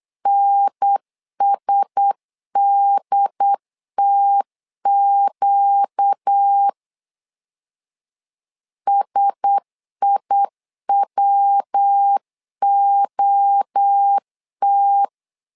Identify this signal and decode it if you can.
Morse code, heard as a keyed tone, about 8 wpm, 800 Hz. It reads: NSDTQ SIWOT
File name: LEZIONI / 4 Q1S 8wpm 2710